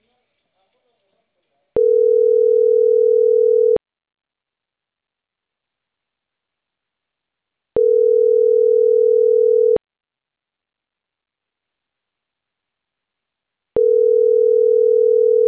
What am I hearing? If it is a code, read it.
Ringback tone (North America), 440 + 480 Hz, with speech.
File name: CENTRAL CAEA / out agcaea01